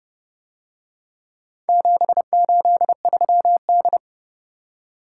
Morse code, heard as a keyed tone, about 30 wpm, 700 Hz. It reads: 783D